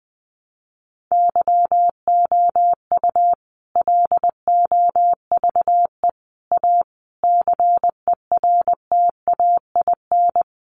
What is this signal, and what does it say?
Morse code, heard as a keyed tone, about 20 wpm, 700 Hz. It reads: YOU LOVE A CERTAIN
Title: Sentences from Top 500 Words 20wpm